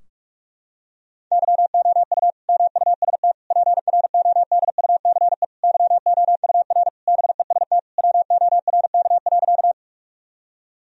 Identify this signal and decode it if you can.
Morse code, heard as a keyed tone, about 41 wpm, 700 Hz. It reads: YOU MUST PRODUCE YOUR BEST WORK.
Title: Sentences from Top 1000 Words - Review Part 3 - 45wpm